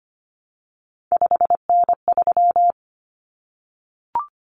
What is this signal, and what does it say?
Morse code, heard as a keyed tone, about 25 wpm, 700 Hz. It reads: 5N3
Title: Common Cut Numbers - 3-Digit 25wpm